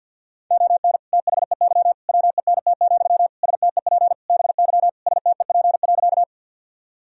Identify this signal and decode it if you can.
Morse code, heard as a keyed tone, about 52 wpm, 700 Hz. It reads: ON THEY WENT, STEP BY STEP.
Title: Sentences from Top 1000 Words - Review Part 2 - 50wpm